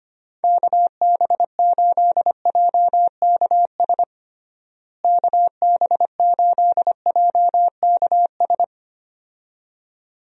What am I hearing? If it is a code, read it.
Morse code, heard as a keyed tone, about 25 wpm, 700 Hz. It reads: KB8JKS KB8JKS
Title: USA Call Signs - 6 Characters - Repeated 2x - 25wpm